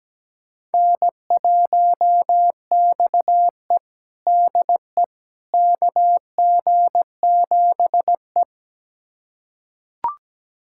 Morse code, heard as a keyed tone, about 17 wpm, 700 Hz. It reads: N1XE DE KG7E
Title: USA Callsign DE Callsign - 4 Characters - 17wpm